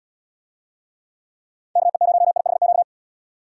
Morse code, heard as a keyed tone, about 76 wpm, 700 Hz. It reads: KE0IHQ